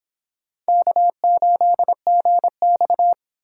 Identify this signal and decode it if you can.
Morse code, heard as a keyed tone, about 26 wpm, 700 Hz. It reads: K8GX